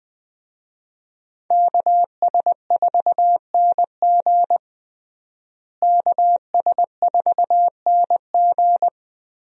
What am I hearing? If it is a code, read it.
Morse code, heard as a keyed tone, about 20 wpm, 700 Hz. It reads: KS4NG KS4NG
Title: USA Call Signs - 5 Characters - Repeated 2x - 20wpm